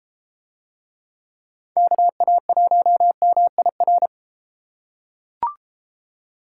Morse code, heard as a keyed tone, about 31 wpm, 700 Hz. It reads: KA1MIR